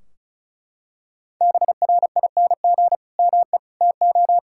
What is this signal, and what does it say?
Morse code, heard as a keyed tone, about 36 wpm, 700 Hz. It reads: BRING ME TO
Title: Sentences from Top 1600 Words 35wpm